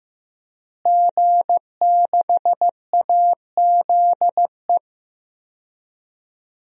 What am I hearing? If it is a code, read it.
Morse code, heard as a keyed tone, about 16 wpm, 700 Hz. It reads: G6AZE
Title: DXCC-100 Call Signs - 5 Characters - Repeated 2x - 15wpm